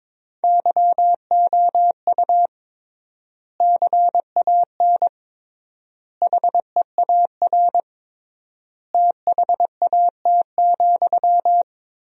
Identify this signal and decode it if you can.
Morse code, heard as a keyed tone, about 22 wpm, 700 Hz. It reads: YOU CAN HEAR THAT,
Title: 3x Word-Spacing - Sentences from Top 500 Words - 22wpm